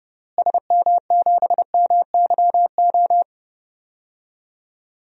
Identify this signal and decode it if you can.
Morse code, heard as a keyed tone, about 30 wpm, 700 Hz. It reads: SM7MYO
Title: DXCC-100 Call Signs - 6 Characters - Repeated 2x - 30wpm